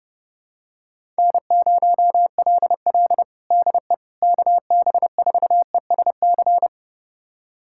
Morse code, heard as a keyed tone, about 32 wpm, 700 Hz. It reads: N0LL DE KB4ESC
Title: USA Callsign DE Callsign 30wpm